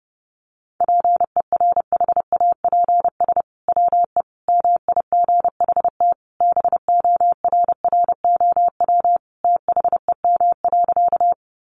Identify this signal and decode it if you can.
Morse code, heard as a keyed tone, about 28 wpm, 700 Hz. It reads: PERHAPS WE MIGHT BORROW THEM.
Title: Sentences from Top 1500 Words 30wpm